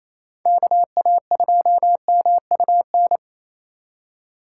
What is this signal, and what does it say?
Morse code, heard as a keyed tone, about 35 wpm, 700 Hz. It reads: KA2MUN